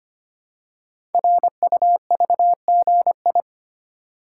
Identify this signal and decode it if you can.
Morse code, heard as a keyed tone, about 25 wpm, 700 Hz. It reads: RUVGI
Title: Code Groups - 5 Letter:Numbers - 25wpm